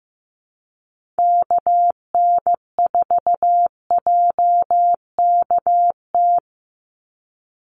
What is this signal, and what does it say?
Morse code, heard as a keyed tone, about 15 wpm, 700 Hz. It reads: KN4JKT
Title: USA Call Signs 15wpm